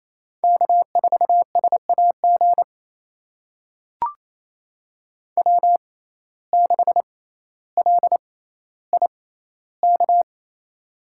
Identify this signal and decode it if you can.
Morse code, heard as a keyed tone, about 28 wpm, spaced as for 14 wpm, 700 Hz. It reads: K4SAG W6LIK
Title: USA Call Signs - 5 Characters - 6x Character Spacing - 28wpm